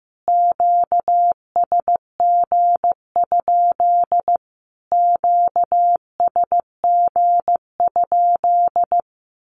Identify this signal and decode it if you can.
Morse code, heard as a keyed tone, about 15 wpm, 700 Hz. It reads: QSG? QSG?
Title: Q-Codes Repeated 3x 15wpm